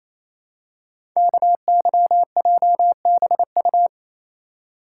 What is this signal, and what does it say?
Morse code, heard as a keyed tone, about 28 wpm, 700 Hz. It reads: KYJBU
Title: Code Groups - 5 Letters - 28wpm